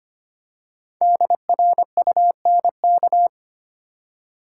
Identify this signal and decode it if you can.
Morse code, heard as a keyed tone, about 25 wpm, 700 Hz. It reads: DRUNK
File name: 5 Letter Words - First 3 Letters Given - 25wpm